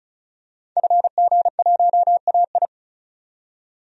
Morse code, heard as a keyed tone, about 34 wpm, 700 Hz. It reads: FG1AI
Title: DXCC-100 Call Signs - 5 Characters - 35wpm